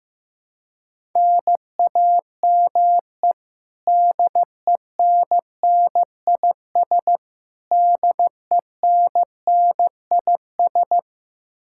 Morse code, heard as a keyed tone, about 15 wpm, 700 Hz. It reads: NAME DENNIS DENNIS
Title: Practice QSO - Exchange 1 - Beginning - 15wpm